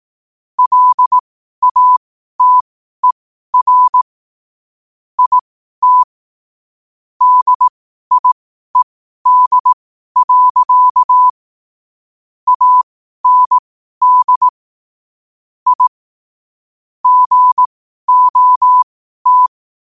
Morse code, heard as a keyed tone, about 18 wpm, spaced as for 12 wpm, 1000 Hz. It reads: LATER IT DIED. AND I GOT
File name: qotdfiles / 12wpm